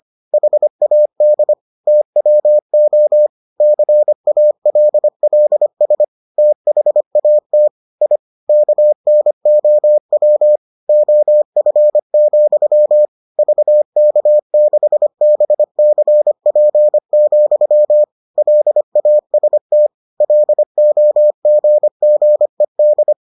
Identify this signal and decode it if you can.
Morse code, heard as a keyed tone, about 25 wpm, 600 Hz. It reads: HAD TWO CALLS THAT I KNOW OF, VK6BCP, LAST LOGGED